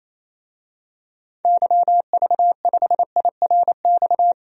Morse code, heard as a keyed tone, about 28 wpm, 700 Hz. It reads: YV5IRX
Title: DXCC-100 Call Signs - 6 Characters - 28wpm